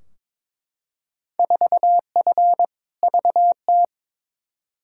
Morse code, heard as a keyed tone, about 22 wpm, 700 Hz. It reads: 4F VT
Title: Field Day Exchanges Encore - 22wpm